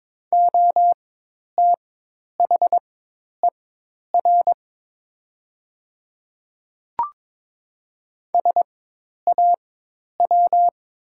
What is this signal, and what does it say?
Morse code, heard as a keyed tone, about 22 wpm, spaced as for 10 wpm, 700 Hz. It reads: OTHER SAW
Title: 4x Character Spacing - Top 300 Words - 22wpm